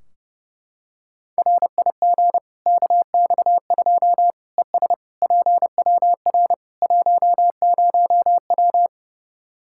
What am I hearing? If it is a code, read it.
Morse code, heard as a keyed tone, about 30 wpm, 700 Hz. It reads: RIG KX2 ES PWR 10W
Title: Practice QSO - Exchange 2 - Info - 30wpm